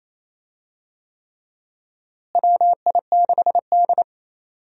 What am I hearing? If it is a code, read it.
Morse code, heard as a keyed tone, about 28 wpm, 700 Hz. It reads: WI6D
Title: USA Call Signs - 4 Characters - Repeated 3x - 28wpm